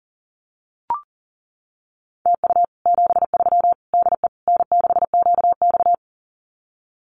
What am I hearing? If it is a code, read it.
Morse code, heard as a keyed tone, about 40 wpm, 700 Hz. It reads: TU 73 DE N6QX
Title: Common QSO Phrases Encore 40wpm